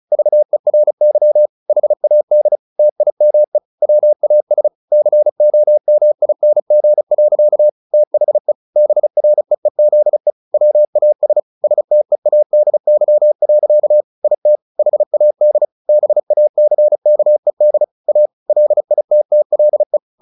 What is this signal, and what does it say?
Morse code, heard as a keyed tone, about 34 wpm, 600 Hz. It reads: VERY HAD TIME WAS COMING. THE BREEZE WAS STEADY. IT HAD BACKED A LITTLE
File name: the old man and the sea 2 35wpm